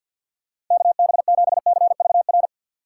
Morse code, heard as a keyed tone, about 47 wpm, 700 Hz. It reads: KB7CVR
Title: USA Call Signs - 6 Characters - 4x Character Spacing - 50wpm